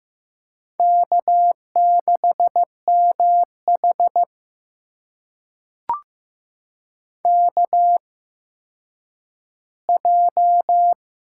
Morse code, heard as a keyed tone, about 15 wpm, spaced as for 9 wpm, 700 Hz. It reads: K6MH KJ
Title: USA Call Signs - 8x Character Spacing - 15wpm